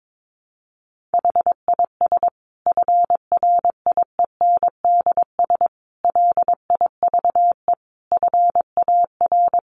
Morse code, heard as a keyed tone, about 22 wpm, 700 Hz. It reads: HIS FRIENDS LIVE FAR
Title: Additional Easy Phrases 22wpm